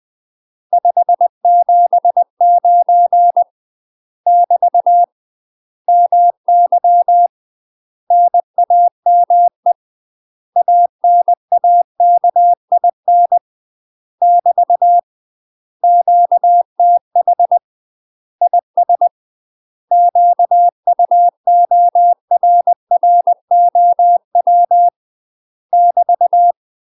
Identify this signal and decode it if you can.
Morse code, heard as a keyed tone, about 20 wpm, 700 Hz. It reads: 579 <BT> MY NAME ANAKIN <BT> QTH IS QUORROW <BT>
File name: UK MASTER QSOs TRAINING / 20WPM / UK MASTER QSOs 0060